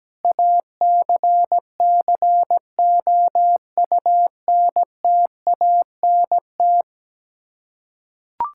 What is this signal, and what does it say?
Morse code, heard as a keyed tone, about 17 wpm, 700 Hz. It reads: ACCOUNTANT
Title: Practice QSO - Exchange 3 - More Info - 17wpm